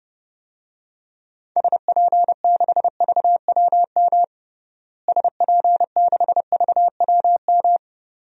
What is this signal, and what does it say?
Morse code, heard as a keyed tone, about 30 wpm, 700 Hz. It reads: SP6VWM SP6VWM
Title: DXCC-100 Call Signs - 6 Characters - Repeated 2x - 30wpm